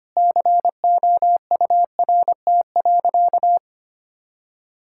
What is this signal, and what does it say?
Morse code, heard as a keyed tone, about 25 wpm, 700 Hz. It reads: COURT.
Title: Sentences from Top 1600 Words 25wpm